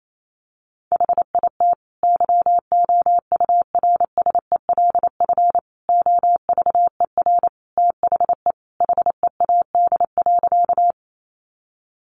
Morse code, heard as a keyed tone, about 28 wpm, 700 Hz. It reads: HIT YOURSELF OVER THE HEAD.
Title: Sentences from Top 1200 Words 28wpm